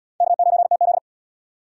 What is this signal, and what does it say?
Morse code, heard as a keyed tone, about 77 wpm, 700 Hz. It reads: K9TZ